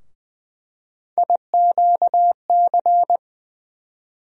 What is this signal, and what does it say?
Morse code, heard as a keyed tone, about 20 wpm, 700 Hz. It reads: IQC
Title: USA Call Sign Postfixes - 3 Letters 20wpm